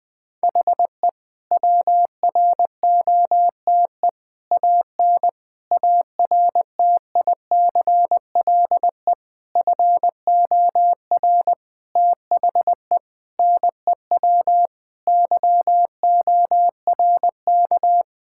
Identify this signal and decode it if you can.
Morse code, heard as a keyed tone, about 20 wpm, 700 Hz. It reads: HE WROTE AN ARTICLE FOR THE NEW YORK
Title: Sentences with 5-Word Phrases - 20wpm